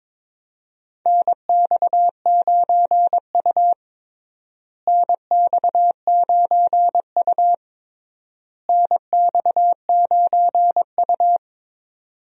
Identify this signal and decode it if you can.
Morse code, heard as a keyed tone, about 22 wpm, 700 Hz. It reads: NX9U NX9U NX9U
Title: USA Call Signs - 4 Characters - Repeated 3x - 22wpm